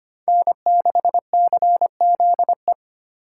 Morse code, heard as a keyed tone, about 32 wpm, 700 Hz. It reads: N6CZE